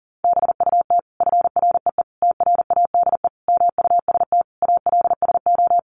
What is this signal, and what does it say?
Morse code, heard as a keyed tone, about 37 wpm, 700 Hz. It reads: BUT FREE TRADE MUST ALSO
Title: Sentences from Top 1300 Words 40wpm